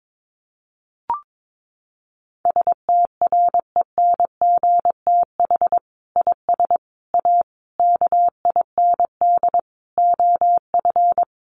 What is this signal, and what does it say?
Morse code, heard as a keyed tone, about 22 wpm, 700 Hz. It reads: STRENGTH IS A KIND OF
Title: Sentences from Top 1300 Words 22wpm